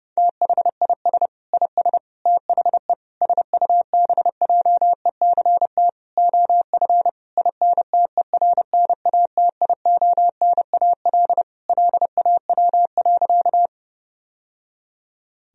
Morse code, heard as a keyed tone, about 32 wpm, 700 Hz. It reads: THIS IS THE SUBJECT OF INTERNATIONAL LAW.
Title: Sentences from Top 1000 Words - Review Part 4 - 30wpm